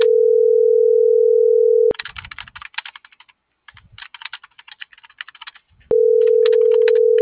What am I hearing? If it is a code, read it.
Ringback tone (North America), 440 + 480 Hz.